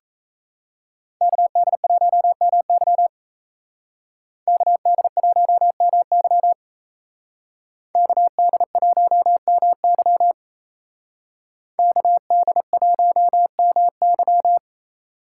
Morse code, heard as a keyed tone, about 36 wpm, 700 Hz. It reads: KD1MY KD1MY KD1MY KD1MY